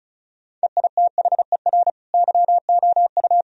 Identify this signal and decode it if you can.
Morse code, heard as a keyed tone, about 36 wpm, 700 Hz. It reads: EITHER YOU
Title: Sentences from Top 1000 Words - Review Part 4 - 35wpm